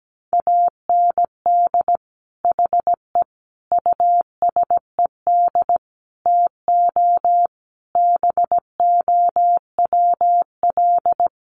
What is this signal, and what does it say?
Morse code, heard as a keyed tone, about 17 wpm, 700 Hz. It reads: AND HE USED TO BOWL
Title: Sentences from Top 1200 Words 17wpm